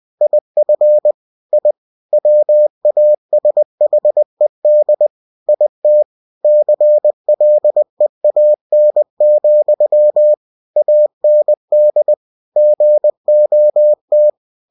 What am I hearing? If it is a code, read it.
Morse code, heard as a keyed tone, about 20 wpm, 600 Hz. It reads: IF I WASHED IT CLEAN, AND GOT
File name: cw-books / winnie-the-pooh / winnie-20wpm-page-0050